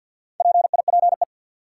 Morse code, heard as a keyed tone, about 53 wpm, 700 Hz. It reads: PIPE
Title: Sentences from Top 1500 Words - New Word Drill - 50wpm